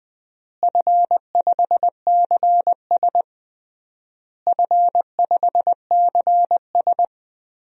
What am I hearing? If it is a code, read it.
Morse code, heard as a keyed tone, about 20 wpm, 700 Hz. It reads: F5CS F5CS
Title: DXCC-100 Call Signs - 4 Characters - Repeated 3x - 20wpm